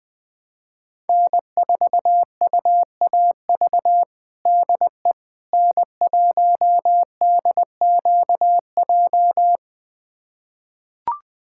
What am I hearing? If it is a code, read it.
Morse code, heard as a keyed tone, about 20 wpm, 700 Hz. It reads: N4UAV DE N1DQJ
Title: USA Callsign DE Callsign - 5 Characters - 20wpm